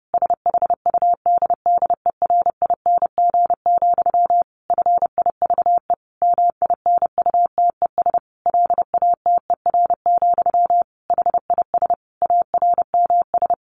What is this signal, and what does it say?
Morse code, heard as a keyed tone, about 30 wpm, 700 Hz. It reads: SHUDDERING, FIVE MINUTES LATER, HIS ARMS
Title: A Sound Of Thunder 30wpm